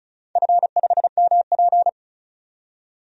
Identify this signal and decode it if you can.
Morse code, heard as a keyed tone, about 36 wpm, 700 Hz. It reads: F5MP
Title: DXCC-100 Call Signs - 4 Characters - 35wpm